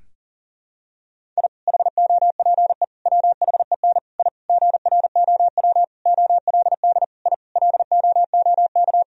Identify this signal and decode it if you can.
Morse code, heard as a keyed tone, about 40 wpm, 700 Hz. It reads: I HOPE WHEN I GROW OLD I LOOK
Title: Sentences from Top 1000 Words - Review Part 1 - 40wpm